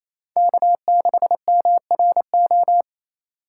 Morse code, heard as a keyed tone, about 28 wpm, 700 Hz. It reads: K6MRO
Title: USA Call Signs - 5 Characters - 6x Character Spacing - 28wpm